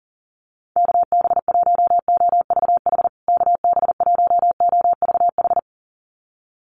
Morse code, heard as a keyed tone, about 40 wpm, 700 Hz. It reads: KB1OVH KB1OVH
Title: USA Call Signs Repeated 2x 40wpm